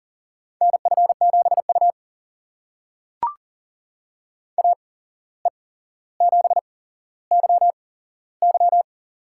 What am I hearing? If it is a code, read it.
Morse code, heard as a keyed tone, about 40 wpm, spaced as for 16 wpm, 700 Hz. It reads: NF7U AE7YY
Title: USA Call Signs - 8x Character Spacing - 40wpm